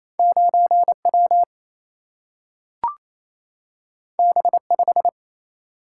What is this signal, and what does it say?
Morse code, heard as a keyed tone, about 28 wpm, 700 Hz. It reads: 9W B5
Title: Single-Letter Single-Number 28wpm